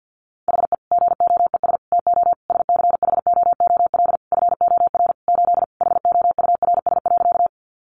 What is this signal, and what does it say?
Morse code, heard as a keyed tone, about 50 wpm, 700 Hz. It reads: HE GOES TO SCHOOL FOR 8 HOURS.
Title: Additional Easy Phrases 50wpm